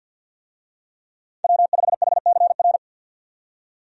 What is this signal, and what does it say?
Morse code, heard as a keyed tone, about 52 wpm, 700 Hz. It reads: W5HCR
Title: USA Call Signs - 5 Characters - 50wpm